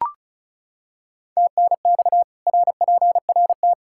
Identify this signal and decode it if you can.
Morse code, heard as a keyed tone, about 36 wpm, 700 Hz. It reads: TNX RPRT
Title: Practice QSO - Exchange 2 - Info - 35wpm